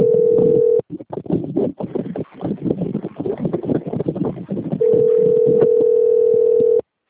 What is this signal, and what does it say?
Ringback tone (North America), 440 + 480 Hz.